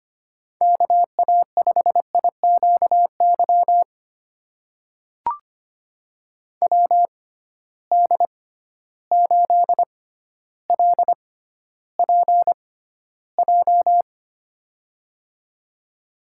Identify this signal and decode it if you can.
Morse code, heard as a keyed tone, about 25 wpm, spaced as for 13 wpm, 700 Hz. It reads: KA5IQY WD8LPJ
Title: USA Call Signs - 6 Characters - 6x Character Spacing - 25wpm